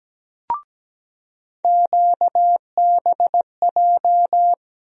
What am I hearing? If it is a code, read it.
Morse code, heard as a keyed tone, about 17 wpm, 700 Hz. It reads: QBJ
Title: USA Call Sign Postfixes - 3 Letters 17wpm